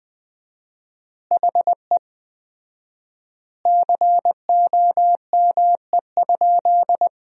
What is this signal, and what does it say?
Morse code, heard as a keyed tone, about 20 wpm, 700 Hz. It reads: HE COME?
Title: 4x Word-Spacing - Sentences from Top 200 Words - 20wpm